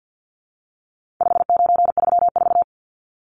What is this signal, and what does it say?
Morse code, heard as a keyed tone, about 48 wpm, 700 Hz. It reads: 5934
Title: Numbers - 4-Digit 50wpm